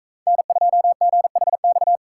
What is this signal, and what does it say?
Morse code, heard as a keyed tone, about 42 wpm, 700 Hz. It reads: N2GHX